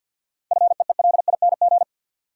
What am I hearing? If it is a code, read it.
Morse code, heard as a keyed tone, about 51 wpm, 700 Hz. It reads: FEELING
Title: Sentences from Top 1200 Words - New Word Drill Alt - 50wpm